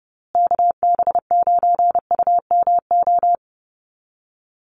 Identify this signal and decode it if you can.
Morse code, heard as a keyed tone, about 30 wpm, 700 Hz. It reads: KB9UMO
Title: USA Call Signs Repeated 2x 30wpm